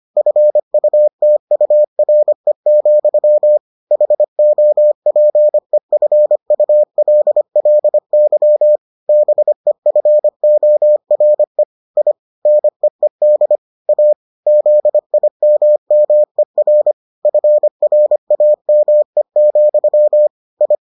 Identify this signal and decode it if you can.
Morse code, heard as a keyed tone, about 25 wpm, 600 Hz. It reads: FUTURE, HOPEFULLY BEFORE I NEED A ZIMMER FRAME, I